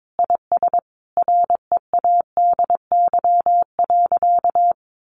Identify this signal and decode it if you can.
Morse code, heard as a keyed tone, about 22 wpm, 700 Hz. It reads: IS READY.
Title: Sentences from Top 700 Words 22wpm